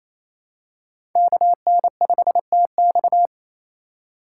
Morse code, heard as a keyed tone, about 28 wpm, 700 Hz. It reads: KN5TX
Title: K1USN SST QSO Practice - 28wpm